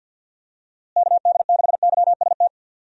Morse code, heard as a keyed tone, about 49 wpm, 700 Hz. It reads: KD6YST
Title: USA Call Signs - 6 Characters - Repeated 3x - 50wpm